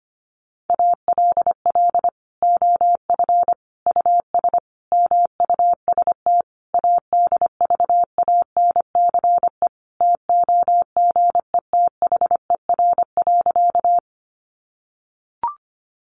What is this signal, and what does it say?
Morse code, heard as a keyed tone, about 25 wpm, 700 Hz. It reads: ALL OF US MUST ADVANCE TOGETHER.
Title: Sentences from Top 1400 Words 25wpm